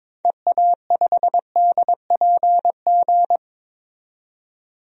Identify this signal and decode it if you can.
Morse code, heard as a keyed tone, about 22 wpm, 700 Hz. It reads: EA5DPG
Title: DXCC-100 Call Signs - 6 Characters - 22wpm